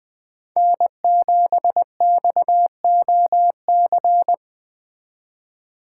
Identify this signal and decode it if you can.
Morse code, heard as a keyed tone, about 20 wpm, 700 Hz. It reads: N7XOC
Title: Speed Racing - USA 1x3 Call Signs 17wpm